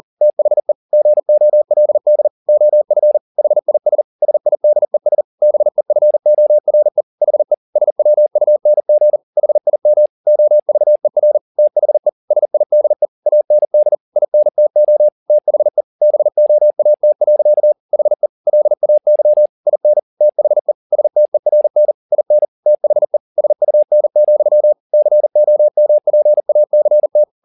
Morse code, heard as a keyed tone, about 40 wpm, 600 Hz. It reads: THE GOLD OF HIS SIDES BEFORE HE SWUNG HIM OVER THE SIDE AND INTO THE BOAT. HE LAY IN THE STERN IN THE SUN, COMPACT